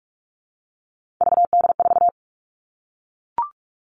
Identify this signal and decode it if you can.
Morse code, heard as a keyed tone, about 44 wpm, 700 Hz. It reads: VD4
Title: All Cut Numbers - 3-Digit 45wpm